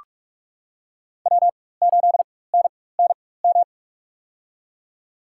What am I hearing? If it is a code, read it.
Morse code, heard as a keyed tone, about 45 wpm, spaced as for 21 wpm, 690 Hz. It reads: W8NNM
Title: USA Call Signs - 5 Characters - 4x Character Spacing - 45wpm